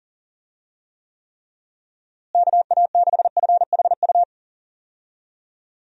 Morse code, heard as a keyed tone, about 40 wpm, 700 Hz. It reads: KA6FHU
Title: USA Call Signs - 6 Characters - Repeated 3x - 40wpm